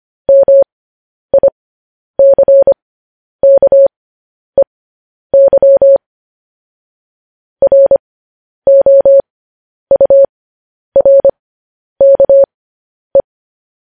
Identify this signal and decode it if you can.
Morse code, heard as a keyed tone, about 25 wpm, spaced as for 10 wpm, 550 Hz. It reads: MICKEY ROURKE
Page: News Headlines in Morse Code at 10 WPM